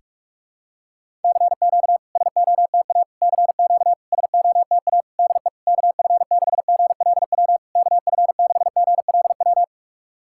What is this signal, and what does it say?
Morse code, heard as a keyed tone, about 44 wpm, 700 Hz. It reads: CQ SOTA CQ SOTA DE KF6GLW KF6GLW
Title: SOTA QSO Practice 45wpm